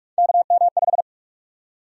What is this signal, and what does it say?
Morse code, heard as a keyed tone, about 46 wpm, 700 Hz. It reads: KM5
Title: USA Call Sign Prefixes - 45wpm